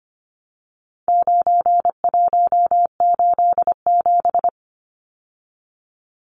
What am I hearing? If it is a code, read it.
Morse code, heard as a keyed tone, about 25 wpm, 700 Hz. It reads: 9187
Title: Numbers - 4-Digit 25wpm